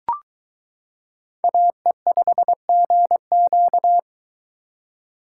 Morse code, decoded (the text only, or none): AE5GQ